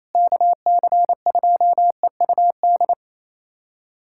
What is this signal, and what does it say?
Morse code, heard as a keyed tone, about 28 wpm, 700 Hz. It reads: KC2EUD
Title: USA Callsign DE Callsign 28wpm